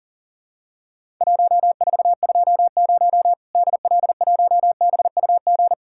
Morse code, heard as a keyed tone, about 40 wpm, 700 Hz. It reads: 1420 DL1BUG
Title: Worked All Europe DX Contest - QTC Exchange - 40wpm